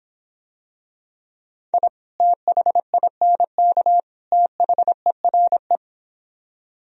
Morse code, heard as a keyed tone, about 26 wpm, 700 Hz. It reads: I THINK THERE